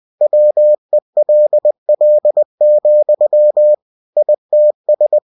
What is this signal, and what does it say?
Morse code, heard as a keyed tone, about 20 wpm, 600 Hz. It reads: WELL, ITS
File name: cw-books / winnie-the-pooh / winnie-20wpm-page-0053